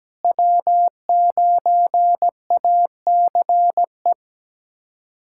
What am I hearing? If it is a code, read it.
Morse code, heard as a keyed tone, about 17 wpm, 700 Hz. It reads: W9ACE
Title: USA Callsign DE Callsign - 5 Characters - 17wpm